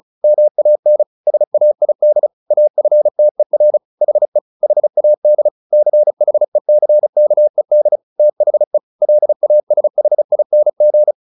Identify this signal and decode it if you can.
Morse code, heard as a keyed tone, about 34 wpm, 600 Hz. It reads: MAN SAID AFTER HE HAD CHECKED THE LASHING
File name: the old man and the sea 2 35wpm